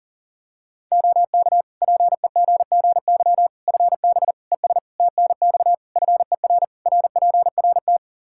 Morse code, heard as a keyed tone, about 40 wpm, 700 Hz. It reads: OK PEGGY FB ES TNX FER RPRT
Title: Practice QSO - Exchange 2 - Info - 40wpm